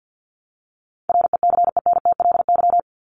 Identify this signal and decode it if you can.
Morse code, heard as a keyed tone, about 50 wpm, 700 Hz. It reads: RECENTLY
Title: LY Suffix - Top 100 Words with Base Words 50wpm